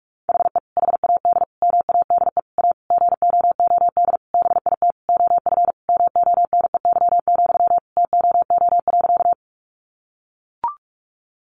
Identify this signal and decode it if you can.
Morse code, heard as a keyed tone, about 44 wpm, 700 Hz. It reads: HE HAD MADE A GOOD BIT OF MONEY, TOO.
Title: Sentences from Top 900 Words 45wpm